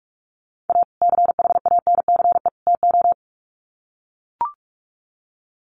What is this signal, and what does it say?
Morse code, heard as a keyed tone, about 44 wpm, 700 Hz. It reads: A CHANCE TO
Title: Sets of 3 Words 45wpm